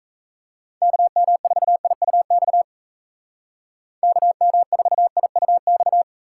Morse code, heard as a keyed tone, about 33 wpm, 700 Hz. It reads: KM4IUX KM4IUX
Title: Speed Racing - USA 2x3 Call Signs 28wpm